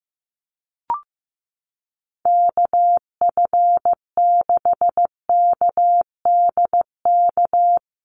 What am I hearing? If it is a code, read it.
Morse code, heard as a keyed tone, about 15 wpm, 700 Hz. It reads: KF6KDK
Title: USA Call Signs Repeated 3x 15wpm